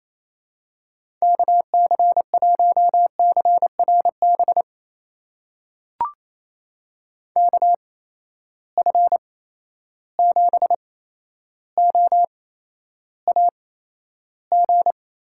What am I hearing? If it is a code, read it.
Morse code, heard as a keyed tone, about 28 wpm, spaced as for 12 wpm, 700 Hz. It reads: KC1CRB KF7OAG
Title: USA Call Signs - 6 Characters - 8x Character Spacing - 28wpm